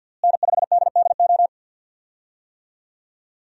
Morse code, heard as a keyed tone, about 52 wpm, 700 Hz. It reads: N5DDO
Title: USA Call Signs - 5 Characters - Repeated 3x - 50wpm